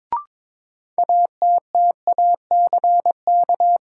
Morse code, heard as a keyed tone, about 22 wpm, 700 Hz. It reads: ATTACK
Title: Sentences from Top 1500 Words - New Word Drill - 22wpm